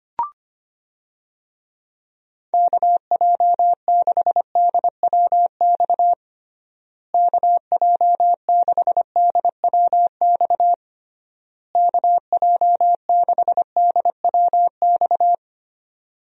Morse code, heard as a keyed tone, about 25 wpm, 700 Hz. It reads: KJ6DWX KJ6DWX KJ6DWX
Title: USA Call Signs - 6 Characters - Repeated 3x - 25wpm